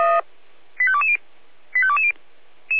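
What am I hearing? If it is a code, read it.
Five-tone selective call (ZVEI), heard as quick runs of five tones, tones of about 70 ms. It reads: 85220 85220
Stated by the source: ZVEI five-tone selective call received on an SDR.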